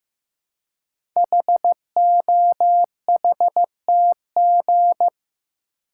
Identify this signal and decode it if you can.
Morse code, heard as a keyed tone, about 15 wpm, 700 Hz. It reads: HOHTG
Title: Code Groups - 5 Letters - 15wpm